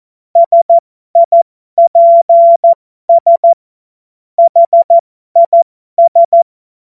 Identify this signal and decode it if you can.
Morse code, heard as a keyed tone, about 14 wpm, spaced as for 12 wpm, 680 Hz. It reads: SIPS HIS
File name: Morsekurs / Witze